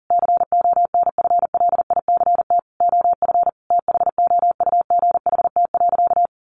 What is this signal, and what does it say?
Morse code, heard as a keyed tone, about 40 wpm, 700 Hz. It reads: CONFLICT OF THOUGHT.
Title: Sentences from Top 1300 Words 40wpm